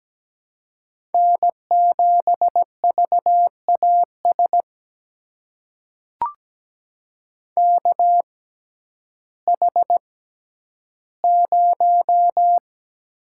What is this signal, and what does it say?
Morse code, heard as a keyed tone, about 17 wpm, spaced as for 10 wpm, 700 Hz. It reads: N7VAS KH0
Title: USA Call Signs - 5 Characters - 6x Character Spacing - 17wpm